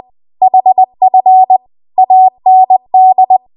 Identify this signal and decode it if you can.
Morse code, heard as a keyed tone, about 20 wpm, 750 Hz. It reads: HF AND
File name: Archive / 20 WPM / 20WPM